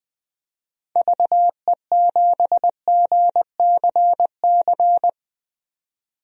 Morse code, heard as a keyed tone, about 21 wpm, 700 Hz. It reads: VE7GCC
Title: DXCC-100 Call Signs - 6 Characters - 20wpm